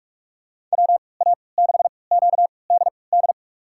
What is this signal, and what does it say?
Morse code, heard as a keyed tone, about 46 wpm, spaced as for 26 wpm, 700 Hz. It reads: WA6QDD
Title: USA Call Signs - 6 Characters - 3x Character Spacing - 45wpm